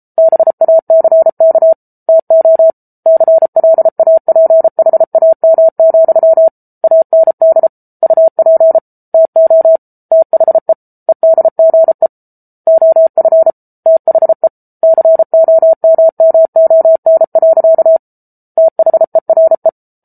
Morse code, heard as a keyed tone, about 34 wpm, 650 Hz. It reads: BACK TO CLAPHAM, AND UP TO THE EDGE OF THE COMMON. THERE